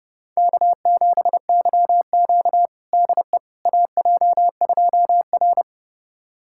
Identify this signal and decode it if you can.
Morse code, heard as a keyed tone, about 30 wpm, 700 Hz. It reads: K7YQ DE AJ2R
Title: USA Callsign DE Callsign - 4 Characters - 30wpm